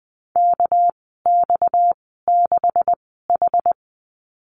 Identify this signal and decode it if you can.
Morse code, heard as a keyed tone, about 20 wpm, spaced as for 14 wpm, 700 Hz. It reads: KX6H
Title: 238 Lesson 38g - Letter q - Call Signs 20wpm